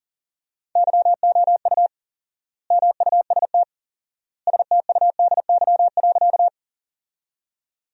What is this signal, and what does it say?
Morse code, heard as a keyed tone, about 40 wpm, 700 Hz. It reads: YOU MUST STUDY.
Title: 4x Word-Spacing - Sentences from Top 300 Words - 40wpm